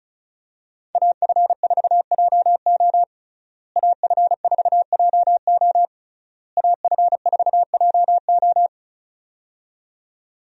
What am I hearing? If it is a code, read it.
Morse code, heard as a keyed tone, about 34 wpm, 700 Hz. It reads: AF4JO AF4JO AF4JO
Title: USA Call Signs - 5 Characters - Repeated 3x - 35wpm